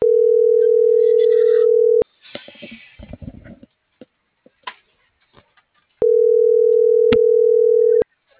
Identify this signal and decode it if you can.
Ringback tone (North America), 440 + 480 Hz.